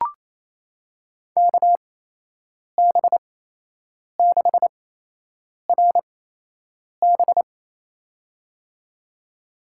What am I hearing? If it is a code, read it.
Morse code, heard as a keyed tone, about 28 wpm, spaced as for 8 wpm, 700 Hz. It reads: KB6RB
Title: USA Call Signs - 5 Characters - 8x Character Spacing - 28wpm